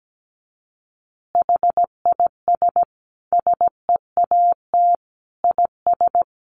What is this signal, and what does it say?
Morse code, heard as a keyed tone, about 17 wpm, 700 Hz. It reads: HIS SEAT IS
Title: Additional Easy Phrases 17wpm